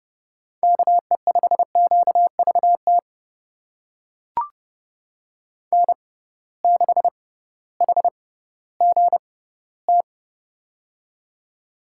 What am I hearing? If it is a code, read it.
Morse code, heard as a keyed tone, about 30 wpm, spaced as for 16 wpm, 700 Hz. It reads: KE5QVT N6HGT
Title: USA Call Signs - 6x Character Spacing - 30wpm